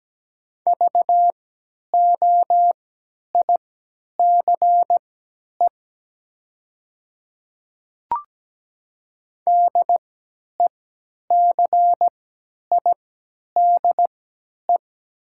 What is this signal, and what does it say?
Morse code, heard as a keyed tone, about 17 wpm, spaced as for 10 wpm, 700 Hz. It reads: VOICE DECIDE
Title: 3x Character Spacing - Top 500 Words - 17wpm